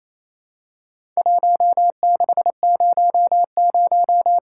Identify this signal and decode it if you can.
Morse code, heard as a keyed tone, about 28 wpm, 700 Hz. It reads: 1600
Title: Worked All Europe DX Contest - QTC Exchange - 28wpm